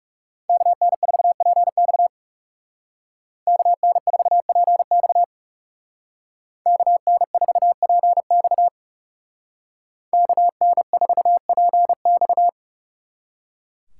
Morse code, heard as a keyed tone, about 37 wpm, 700 Hz. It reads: KN4PX KN4PX KN4PX KN4PX
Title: Speed Racing - USA Call Signs 30wpm